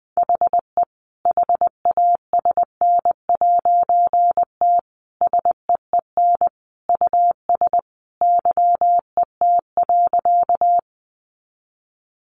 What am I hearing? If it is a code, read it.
Morse code, heard as a keyed tone, about 20 wpm, 700 Hz. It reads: HE HASN'T SEEN US YET.
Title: A Sound Of Thunder 20wpm